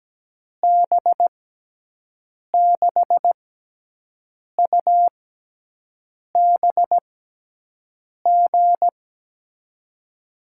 Morse code, heard as a keyed tone, about 17 wpm, spaced as for 6 wpm, 700 Hz. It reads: B6UBG
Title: USA Call Signs - 6x Character Spacing - 17wpm